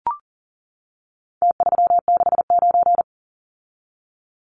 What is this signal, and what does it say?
Morse code, heard as a keyed tone, about 40 wpm, 700 Hz. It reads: T369
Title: Numbers - Contest Style - 4-Digit 40wpm